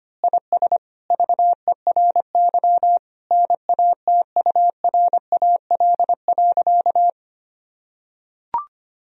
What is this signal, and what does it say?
Morse code, heard as a keyed tone, about 25 wpm, 700 Hz. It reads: IS VERY NATURAL.
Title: Sentences from Top 1000 Words - Review Part 4 - 25wpm